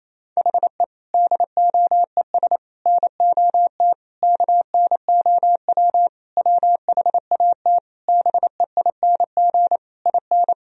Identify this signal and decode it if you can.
Morse code, heard as a keyed tone, about 28 wpm, 700 Hz. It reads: HE DOES NOT KNOW WHAT BEING IN
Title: Sentences from Top 1000 Words - Review Part 3 - 28wpm